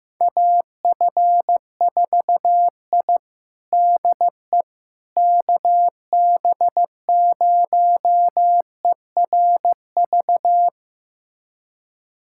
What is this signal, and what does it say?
Morse code, heard as a keyed tone, about 15 wpm, 700 Hz. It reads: AF4I DE KB0ERV
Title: USA Callsign DE Callsign 15wpm